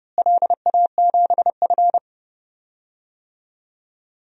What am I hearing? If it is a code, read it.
Morse code, heard as a keyed tone, about 31 wpm, 700 Hz. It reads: LA7F
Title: DXCC-100 Call Signs - 4 Characters - Repeated 2x - 30wpm